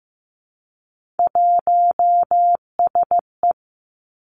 Morse code, heard as a keyed tone, about 15 wpm, 700 Hz. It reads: 1SE
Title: Three Challenging Letter-Number Combinations 15wpm